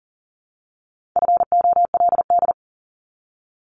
Morse code, heard as a keyed tone, about 40 wpm, 700 Hz. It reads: FOLD
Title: Top 1751-2000 Words 40wpm